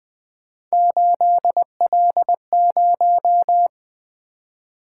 Morse code, heard as a keyed tone, about 20 wpm, 700 Hz. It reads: 8L0
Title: Three Letter-Number Combinations - 20wpm